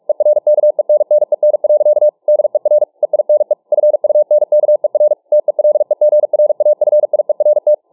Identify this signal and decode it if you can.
Morse code, heard as a keyed tone, about 44 wpm, 600 Hz. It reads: ERKENNEN. DER EINE FUNKER TELEGRAFIERT
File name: dk5ke / Morsepsycho 45WpM